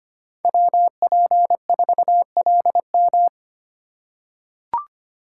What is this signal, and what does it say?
Morse code, heard as a keyed tone, about 25 wpm, 700 Hz. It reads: WP4LM
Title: USA Call Signs - 5 Characters - 4x Character Spacing - 25wpm